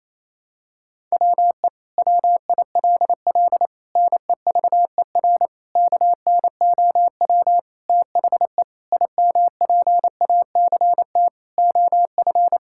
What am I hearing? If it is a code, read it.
Morse code, heard as a keyed tone, about 28 wpm, 690 Hz. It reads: WE WILL NEVER KNOW THE IMPACT OF
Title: Sentences from Top 1000 Words - Review Part 6 - 28wpm